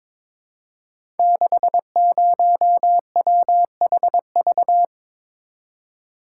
Morse code, heard as a keyed tone, about 22 wpm, 700 Hz. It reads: 60WHV
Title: Code Groups - 5 Letter:Numbers - 22wpm